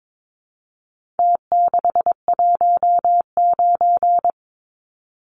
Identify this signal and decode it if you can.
Morse code, heard as a keyed tone, about 22 wpm, 700 Hz. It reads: T619
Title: Numbers - Contest Style - 4-Digit 22wpm